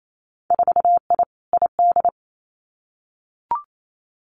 Morse code, heard as a keyed tone, about 28 wpm, 700 Hz. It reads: 4I ID
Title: Winter Field Day Exchanges 28wpm